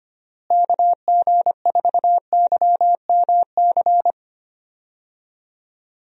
Morse code, heard as a keyed tone, about 25 wpm, 700 Hz. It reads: KG4YMC
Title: USA Call Signs - 6 Characters - Repeated 2x - 25wpm